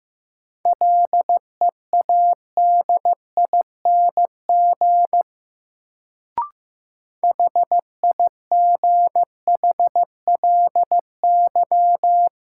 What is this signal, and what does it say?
Morse code, heard as a keyed tone, about 15 wpm, 700 Hz. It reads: LEADING HIGHLY
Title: Sentences from Top 1100 Words - New Word Drill Alt - 15wpm